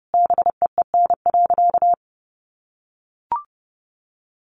Morse code, heard as a keyed tone, about 28 wpm, 700 Hz. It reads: BEEN.
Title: Sentences from Top 300 Words 30wpm